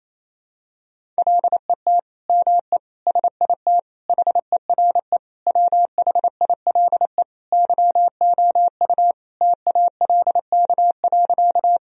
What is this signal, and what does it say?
Morse code, heard as a keyed tone, about 28 wpm, 700 Hz. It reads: LET ME SIT HERE WHILE YOU TALK.
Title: Sentences from Top 1000 Words - Review Part 1 - 28wpm